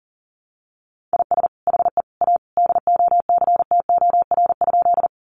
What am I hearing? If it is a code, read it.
Morse code, heard as a keyed tone, about 40 wpm, 700 Hz. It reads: IS HE A DOCTOR?
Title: Sentences from Top 1500 Words 40wpm